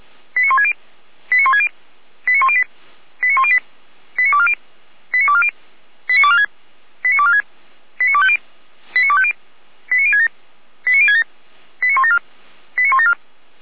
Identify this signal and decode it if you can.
Five-tone selective call (ZVEI), heard as quick runs of five tones, tones of about 70 ms. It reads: 89160 89160 89108 89108 89240 89240 89246 89246 89250 89250 89067 89067 89174 89174